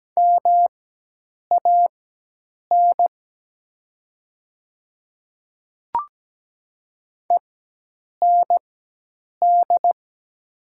Morse code, heard as a keyed tone, about 17 wpm, spaced as for 8 wpm, 700 Hz. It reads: MAN END
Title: 4x Character Spacing - Top 300 Words - 17wpm